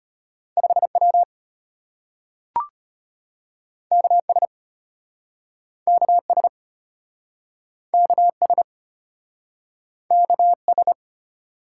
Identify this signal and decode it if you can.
Morse code, heard as a keyed tone, about 34 wpm, 700 Hz. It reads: 5W KS KS KS KS